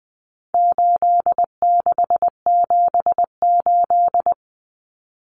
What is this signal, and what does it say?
Morse code, heard as a keyed tone, about 20 wpm, 700 Hz. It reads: 8678
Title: Numbers - 4-Digit 20wpm